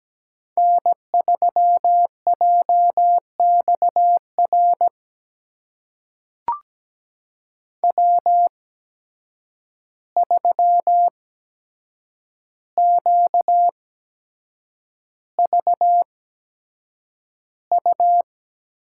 Morse code, heard as a keyed tone, about 17 wpm, spaced as for 7 wpm, 700 Hz. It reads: N3JXR W3QVU
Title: USA Call Signs - 5 Characters - 8x Character Spacing - 17wpm